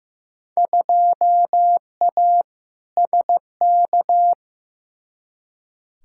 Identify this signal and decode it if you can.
Morse code, heard as a keyed tone, about 15 wpm, 700 Hz. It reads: 2A SK
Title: Field Day Exchanges Encore - 15wpm